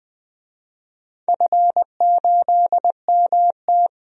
Morse code, heard as a keyed tone, about 20 wpm, 700 Hz. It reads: F8MT
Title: DXCC-100 Call Signs - 4 Characters - 20wpm